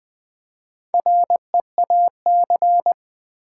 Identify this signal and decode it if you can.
Morse code, heard as a keyed tone, about 20 wpm, 700 Hz. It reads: REAC
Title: Build A Word - 2 Letters At A Time - 8-Letter Words - 20wpm